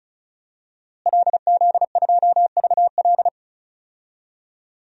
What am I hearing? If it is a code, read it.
Morse code, heard as a keyed tone, about 34 wpm, 700 Hz. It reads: LZ2VL
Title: DXCC-100 Call Signs - 5 Characters - Repeated 3x - 35wpm